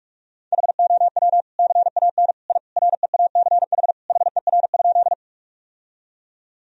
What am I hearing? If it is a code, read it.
Morse code, heard as a keyed tone, about 41 wpm, 700 Hz. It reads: HOW CAN I REACH HER?
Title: Sentences from Top 1000 Words - Review Part 2 - 45wpm